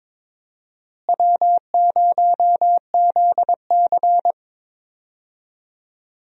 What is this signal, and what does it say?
Morse code, heard as a keyed tone, about 22 wpm, 700 Hz. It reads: W0ZC
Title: USA Call Signs - 4 Characters - Repeated 3x - 22wpm